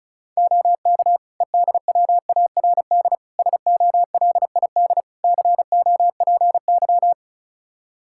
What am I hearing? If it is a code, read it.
Morse code, heard as a keyed tone, about 34 wpm, 700 Hz. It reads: OK EDWARD SOLID COPY
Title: Practice QSO - Exchange 3 - More Info - 35wpm